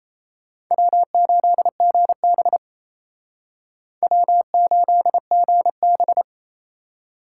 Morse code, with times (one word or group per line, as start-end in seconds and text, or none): 0.7-2.6 W8GB
4.0-6.2 W8GB